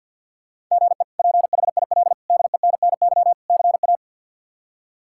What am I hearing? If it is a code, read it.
Morse code, heard as a keyed tone, about 50 wpm, 700 Hz. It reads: GE PHIL DENNY CA
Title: K1USN SST QSO Practice - 50wpm